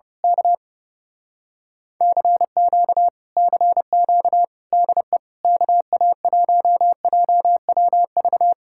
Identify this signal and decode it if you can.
Morse code, heard as a keyed tone, about 29 wpm, 700 Hz. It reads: K CQ CQ DE KA1JWV
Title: Speed-Racing Common QSO Phrases - 30wpm